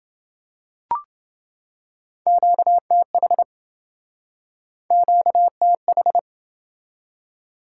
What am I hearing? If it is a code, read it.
Morse code, heard as a keyed tone, about 29 wpm, 700 Hz. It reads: QTH QTH